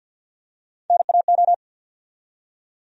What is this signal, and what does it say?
Morse code, heard as a keyed tone, about 53 wpm, 700 Hz. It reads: NAO
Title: USA Call Sign Postfixes - 3 Letters 50wpm